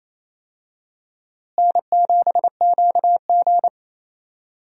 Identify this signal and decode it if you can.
Morse code, heard as a keyed tone, about 28 wpm, 700 Hz. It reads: N7QG
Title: USA Call Signs - 4 Characters - Repeated 2x - 28wpm